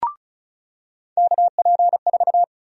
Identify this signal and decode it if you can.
Morse code, heard as a keyed tone, about 36 wpm, 700 Hz. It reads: KP4
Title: USA Call Sign Prefixes - 35wpm